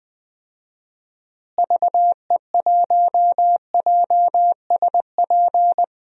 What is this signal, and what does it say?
Morse code, heard as a keyed tone, about 20 wpm, 700 Hz. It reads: VE1JSP
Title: DXCC-100 Call Signs - 6 Characters - Repeated 2x - 20wpm